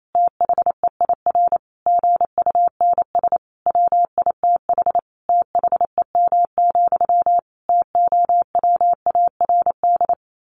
Morse code, heard as a keyed tone, about 28 wpm, 700 Hz. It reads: THEIR GUNS WITH THEM, TOWARD
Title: A Sound Of Thunder 28wpm